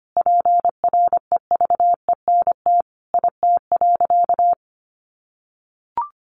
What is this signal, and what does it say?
Morse code, heard as a keyed tone, about 25 wpm, 700 Hz. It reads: PREVENT IT.
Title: Sentences from Top 1100 Words 25wpm